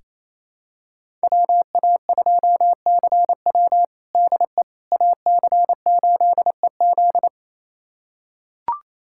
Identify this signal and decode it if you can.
Morse code, heard as a keyed tone, about 28 wpm, 700 Hz. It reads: WA2CW DE AC8EZ
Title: USA Callsign DE Callsign - 5 Characters - 28wpm